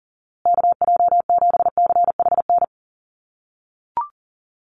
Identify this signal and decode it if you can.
Morse code, heard as a keyed tone, about 40 wpm, 700 Hz. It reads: KJ7CHN
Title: USA Call Signs Repeated 3x 40wpm